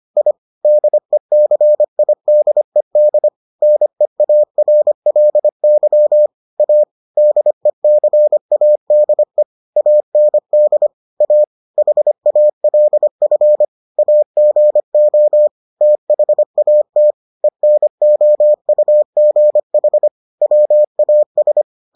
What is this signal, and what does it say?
Morse code, heard as a keyed tone, about 25 wpm, 600 Hz. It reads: I DECIDED NEARLY A DECADE AND A HALF AGO THAT ENOUGH WAS